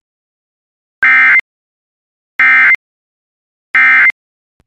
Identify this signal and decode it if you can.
Emergency Alert System (SAME) tones, heard as harsh buzzing bursts. It is heard from 1.0 to 4.1 s.
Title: Relaying Severe Thunderstorm Warning from KOD/DAS.